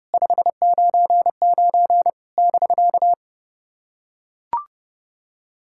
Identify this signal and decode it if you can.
Morse code, heard as a keyed tone, about 30 wpm, 700 Hz. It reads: 599 <BK>
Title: SOTA QSO Practice 30wpm